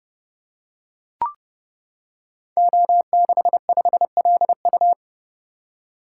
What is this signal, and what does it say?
Morse code, heard as a keyed tone, about 29 wpm, 700 Hz. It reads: O65LU
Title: Code Groups - 5 Letter:Numbers - 30wpm